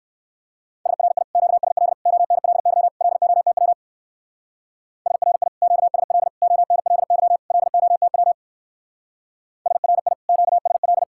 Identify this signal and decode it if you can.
Morse code, heard as a keyed tone, about 67 wpm, 700 Hz. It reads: SRI QSL ONLY LOTW SRI QSL ONLY LOTW SRI QSL